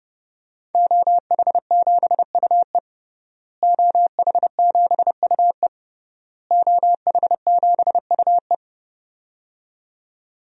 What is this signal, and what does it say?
Morse code, heard as a keyed tone, about 31 wpm, 700 Hz. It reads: OH7UE OH7UE OH7UE
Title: DXCC-100 Call Signs - 5 Characters - Repeated 3x - 30wpm